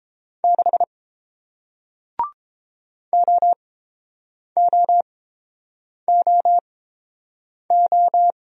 Morse code, heard as a keyed tone, about 30 wpm, spaced as for 7 wpm, 700 Hz. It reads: 6 OOOO